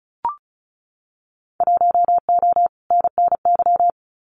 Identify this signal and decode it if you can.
Morse code, heard as a keyed tone, about 34 wpm, 700 Hz. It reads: 1O NNY
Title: Winter Field Day Exchanges 35wpm